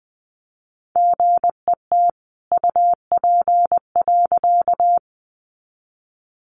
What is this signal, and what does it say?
Morse code, heard as a keyed tone, about 20 wpm, 700 Hz. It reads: GET UP.
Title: A Sound Of Thunder 20wpm